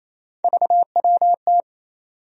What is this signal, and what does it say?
Morse code, heard as a keyed tone, about 28 wpm, 700 Hz. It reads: VWT
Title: USA Call Sign Postfixes - 3 Letters 28wpm